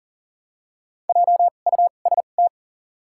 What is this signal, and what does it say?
Morse code, heard as a keyed tone, about 40 wpm, spaced as for 29 wpm, 700 Hz. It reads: JUST
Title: 2x Character Spacing - Top 200 Words - 40wpm